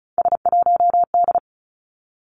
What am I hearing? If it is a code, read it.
Morse code, heard as a keyed tone, about 36 wpm, 700 Hz. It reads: S1D
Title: Three Challenging Letter-Number Combinations 35wpm